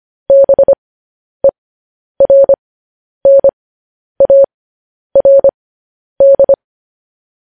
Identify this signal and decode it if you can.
Morse code, heard as a keyed tone, about 25 wpm, spaced as for 10 wpm, 550 Hz. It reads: BERNARD
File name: podcast / News headlines-1x-25x10